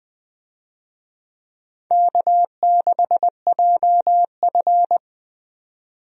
Morse code, heard as a keyed tone, about 20 wpm, 700 Hz. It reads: K6JF
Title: USA Call Signs - 4 Characters - 20wpm